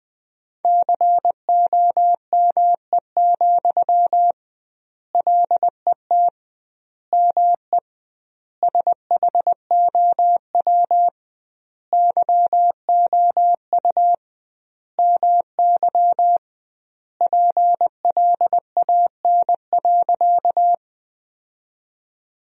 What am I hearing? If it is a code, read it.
Morse code, heard as a keyed tone, about 21 wpm, 700 Hz. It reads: COME, LET ME SHOW YOU MY PLAN.
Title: 2x Word-Spacing - Sentences from Top 500 Words - 20wpm